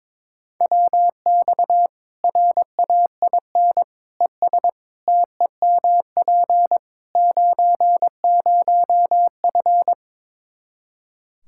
Morse code, heard as a keyed tone, about 22 wpm, 700 Hz. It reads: WX RAIN ES TEMP 90F
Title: Practice QSO - Exchange 2 - Info - 22wpm